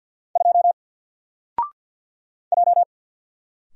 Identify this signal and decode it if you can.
Morse code, heard as a keyed tone, about 48 wpm, 700 Hz. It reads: 2 J
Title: Single Letter-Number Alt 50wpm